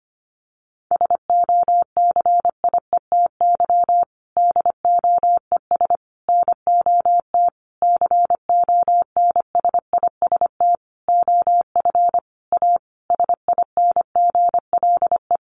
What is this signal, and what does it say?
Morse code, heard as a keyed tone, about 25 wpm, 700 Hz. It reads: SOCIETY DOES NOT CONSIST OF A SINGLE
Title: Sentences from Top 1400 Words 25wpm